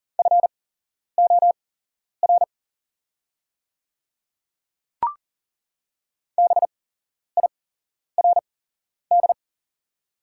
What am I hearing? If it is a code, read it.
Morse code, heard as a keyed tone, about 40 wpm, spaced as for 11 wpm, 700 Hz. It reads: FOR BIRD
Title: 8x Character Spacing - Top 500 Words - 40wpm